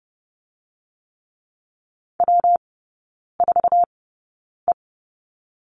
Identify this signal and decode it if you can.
Morse code, heard as a keyed tone, about 30 wpm, 700 Hz. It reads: W 4 E
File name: Lesson 19e - Letter w - Three Characters 30wpm